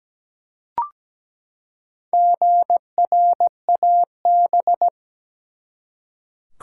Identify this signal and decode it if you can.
Morse code, heard as a keyed tone, about 17 wpm, 700 Hz. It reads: GRAB
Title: Four Letter Words - 17wpm